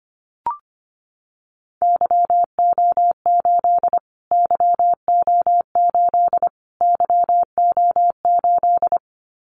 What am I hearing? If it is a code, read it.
Morse code, heard as a keyed tone, about 25 wpm, 700 Hz. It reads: YO8 YO8 YO8
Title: Callsign Prefixes for Top 25 DXCC - Repeated 3x - 25wpm